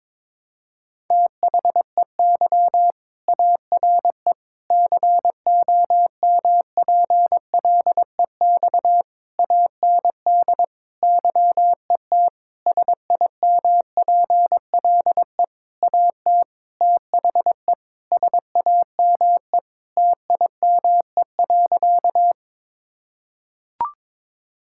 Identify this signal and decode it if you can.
Morse code, heard as a keyed tone, about 22 wpm, 700 Hz. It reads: THEY ARE COMPLEX AND YET SIMPLE AT THE SAME TIME.
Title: Sentences from Top 1000 Words - Review Part 6 - 22wpm